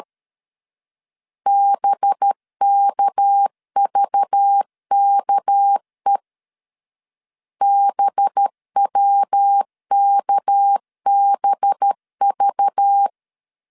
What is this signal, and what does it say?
Morse code, heard as a keyed tone, about 13 wpm, 800 Hz. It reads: BKVKE BWKBV
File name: LEZIONI / 6 PKX0 / 6 PK 12wpm 2006